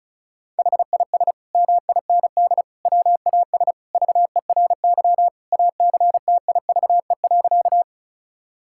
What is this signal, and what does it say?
Morse code, heard as a keyed tone, about 36 wpm, 700 Hz. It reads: HIS MIND WAS VERY ACTIVE.
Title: Sentences from Top 1000 Words - Review Part 6 - 35wpm